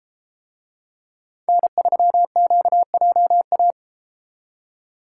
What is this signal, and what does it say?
Morse code, heard as a keyed tone, about 33 wpm, 700 Hz. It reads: N3QJA